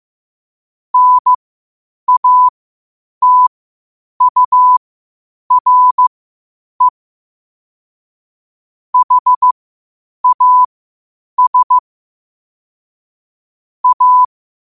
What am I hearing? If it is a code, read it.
Morse code, heard as a keyed tone, about 15 wpm, spaced as for 8 wpm, 1000 Hz. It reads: NATURE HAS A
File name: qotdfiles / 8wpm